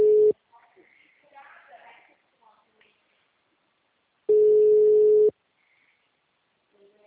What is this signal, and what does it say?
Ringback tone (Europe, 425 Hz), with speech.
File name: out- -2024- - - .45004